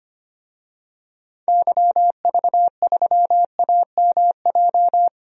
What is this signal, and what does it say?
Morse code, heard as a keyed tone, about 25 wpm, 700 Hz. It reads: YV3AMJ
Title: DXCC-100 Call Signs - 6 Characters - 25wpm